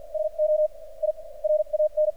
Morse code, heard as a keyed tone, about 46 wpm, spaced as for 39 wpm, 610 Hz. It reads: SO I RAN
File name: archived / CRSnet